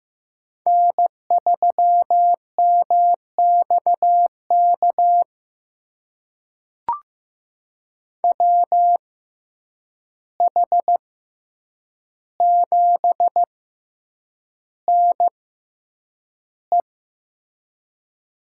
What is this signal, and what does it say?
Morse code, heard as a keyed tone, about 15 wpm, spaced as for 8 wpm, 700 Hz. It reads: N3MXK WH7NE
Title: USA Call Signs - 5 Characters - 6x Character Spacing - 15wpm